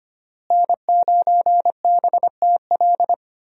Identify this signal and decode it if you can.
Morse code, heard as a keyed tone, about 25 wpm, 700 Hz. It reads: N9BTL
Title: USA Call Signs - 5 Characters - Repeated 3x - 25wpm